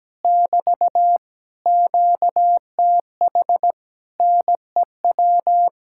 Morse code, heard as a keyed tone, about 17 wpm, 700 Hz. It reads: <BT> QTH NEW
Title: Speed-Racing Common QSO Phrases - 17wpm